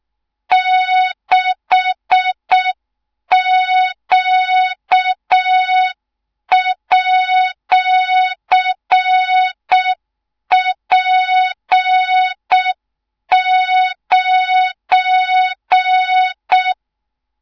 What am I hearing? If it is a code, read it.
Morse code, heard as a keyed tone, about 6 wpm, 3700 Hz, with noise. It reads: 6Q@P9
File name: Morze-HA2MN / hang / 24 GYAKORLAT